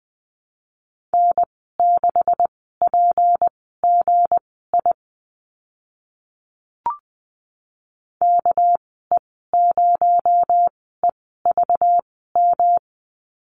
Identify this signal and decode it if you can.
Morse code, heard as a keyed tone, about 20 wpm, spaced as for 14 wpm, 700 Hz. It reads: N6PGI KE0EVM
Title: 245 Lesson 39g - Letter z - Call Signs 20wpm